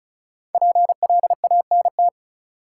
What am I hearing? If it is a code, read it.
Morse code, heard as a keyed tone, about 36 wpm, 700 Hz. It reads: PLANT
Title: Sentences from Top 1000 Words - Review Part 1 - New Word Drill - 35wpm